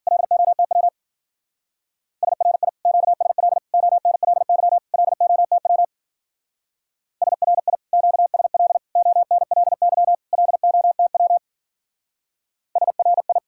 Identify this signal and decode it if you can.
Morse code, heard as a keyed tone, about 47 wpm, 700 Hz. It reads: LOTW SRI QSL ONLY LOTW SRI QSL ONLY LOTW SRI
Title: Speed-Racing Common QSO Phrases Encore - 40wpm